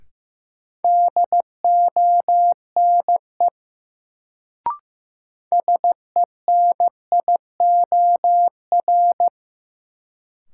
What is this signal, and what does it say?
Morse code, heard as a keyed tone, about 15 wpm, 700 Hz. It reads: DONE SENIOR
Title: Sets of 5 Words Encore - New Word Drill - 15wpm